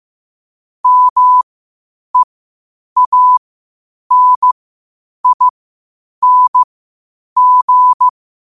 Morse code, heard as a keyed tone, about 15 wpm, spaced as for 8 wpm, 1000 Hz. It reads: MEANING
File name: qotdfiles / 8wpm